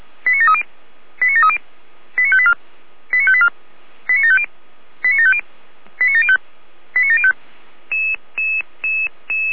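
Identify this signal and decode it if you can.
Five-tone selective call (ZVEI), heard as quick runs of five tones, tones of about 70 ms. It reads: 89730 89730 89584 89584 89750 89750 89705 89705
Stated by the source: Five-tone ZVEI selective call received on an SDR, from the Feuerwehr Orlamünde fire brigade server.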